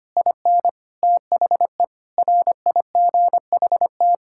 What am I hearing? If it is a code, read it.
Morse code, heard as a keyed tone, about 25 wpm, 700 Hz. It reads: IN THE RIGHT
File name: Sets of 9 Words - 25wpm